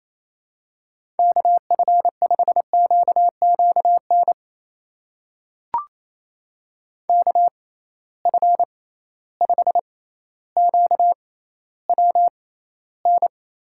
Morse code, heard as a keyed tone, about 28 wpm, spaced as for 14 wpm, 700 Hz. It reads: KF5QQN KF5QWN
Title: USA Call Signs - 6 Characters - 6x Character Spacing - 28wpm